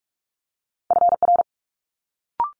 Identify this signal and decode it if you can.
Morse code, heard as a keyed tone, about 46 wpm, 700 Hz. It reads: FR
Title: Multi-letter Phonemes Quick Fire 45wpm